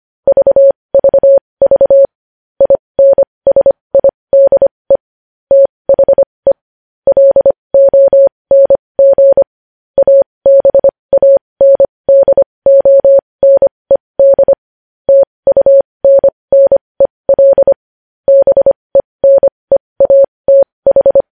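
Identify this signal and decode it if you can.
Morse code, heard as a keyed tone, about 25 wpm, spaced as for 20 wpm, 550 Hz. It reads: VVV INSIDE THE LONG ABANDONED TUNNEL BENEATH
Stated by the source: Morse code news headlines practice at 20 wpm.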